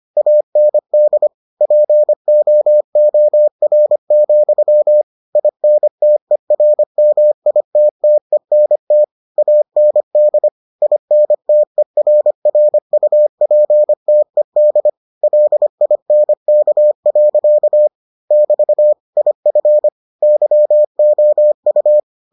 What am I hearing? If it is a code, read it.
Morse code, heard as a keyed tone, about 25 wpm, 600 Hz. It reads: AND POOR, INTERMITTENT AND INTERRUPTED LINK. <BT> IF YOU